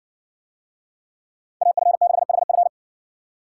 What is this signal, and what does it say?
Morse code, heard as a keyed tone, about 60 wpm, 700 Hz. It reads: AF6HF